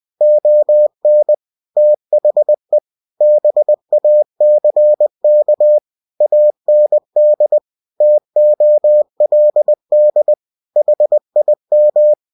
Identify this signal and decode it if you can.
Morse code, heard as a keyed tone, about 20 wpm, 600 Hz. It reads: ON THE BACK AND TOLD HIM
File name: cw-books / winnie-the-pooh / winnie-20wpm-page-0074